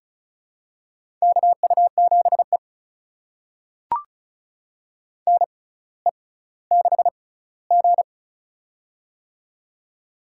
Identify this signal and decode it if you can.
Morse code, heard as a keyed tone, about 34 wpm, spaced as for 18 wpm, 700 Hz. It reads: KU7E NE6G
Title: USA Call Signs - 4 Characters - 6x Character Spacing - 35wpm